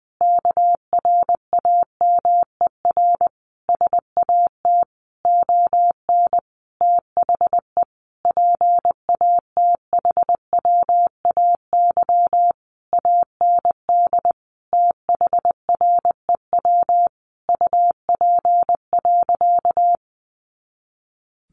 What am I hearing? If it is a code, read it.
Morse code, heard as a keyed tone, about 20 wpm, 700 Hz. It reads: KRAMER SAT ON THE PATHWAY AND THREW UP.
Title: A Sound Of Thunder 20wpm